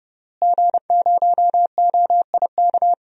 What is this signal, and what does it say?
Morse code, heard as a keyed tone, about 32 wpm, 700 Hz. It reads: G0OIK
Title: DXCC-100 Call Signs - 5 Characters - Repeated 2x - 30wpm